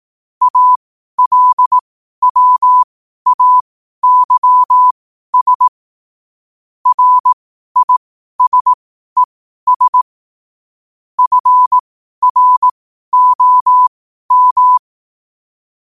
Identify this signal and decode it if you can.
Morse code, heard as a keyed tone, about 18 wpm, spaced as for 12 wpm, 1000 Hz. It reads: ALWAYS RISES FROM
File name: qotdfiles / 12wpm